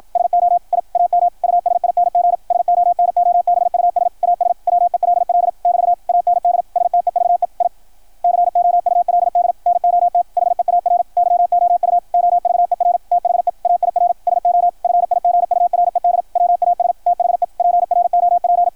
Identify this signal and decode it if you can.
Morse code, heard as a keyed tone, about 53 wpm, 690 Hz, with radio noise. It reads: SO I AM USING SONOBUS AS WELL <BT> AND STEVE I COULD NOT HEAR YOU OVER THE AIR SO FIGURED WAS THE PROP
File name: archived / CRSnet